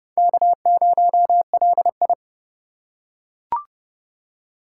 Morse code, heard as a keyed tone, about 25 wpm, 700 Hz. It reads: K0LI